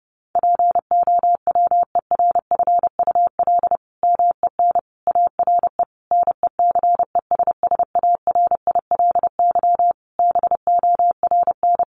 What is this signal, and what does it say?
Morse code, heard as a keyed tone, about 30 wpm, 700 Hz. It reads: POWERFUL MEN ARE NECESSARILY BORN
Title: Sentences from Top 1400 Words 30wpm